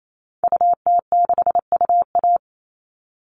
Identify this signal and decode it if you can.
Morse code, heard as a keyed tone, about 28 wpm, 700 Hz. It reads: UT6UA
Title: Call Signs Repeated 3x - 28wpm